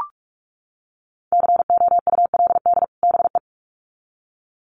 Morse code, heard as a keyed tone, about 44 wpm, 700 Hz. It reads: COULD BE
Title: Sets of 2 Words 45wpm